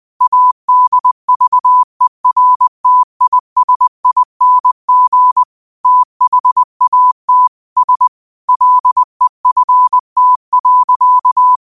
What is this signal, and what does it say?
Morse code, heard as a keyed tone, about 20 wpm, 1000 Hz. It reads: ADVERTISING THAT S LEFT.